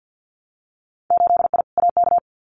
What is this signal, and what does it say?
Morse code, heard as a keyed tone, about 50 wpm, 700 Hz. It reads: 8I AK
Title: Winter Field Day Exchanges Encore 50wpm